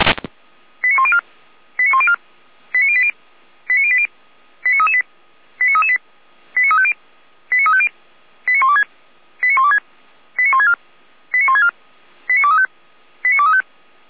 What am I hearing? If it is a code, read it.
Five-tone selective call (ZVEI), heard as quick runs of five tones, tones of about 70 ms. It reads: 89104 89104 89080 89080 89338 89338 89350 89350 89126 89126 89164 89164 89235 89235